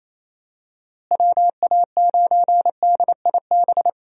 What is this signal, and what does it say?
Morse code, heard as a keyed tone, about 28 wpm, 700 Hz. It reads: WA9DIB
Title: USA Call Signs - 6 Characters - Repeated 2x - 28wpm